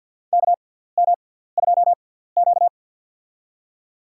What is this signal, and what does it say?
Morse code, heard as a keyed tone, about 47 wpm, spaced as for 17 wpm, 700 Hz. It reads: KM2Q
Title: USA Call Signs - 4 Characters - 6x Character Spacing - 50wpm